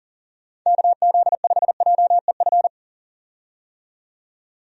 Morse code, heard as a keyed tone, about 40 wpm, 700 Hz. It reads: KZ5JEF
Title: USA Call Signs - 6 Characters - 40wpm